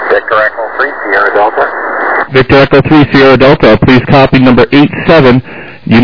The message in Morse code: none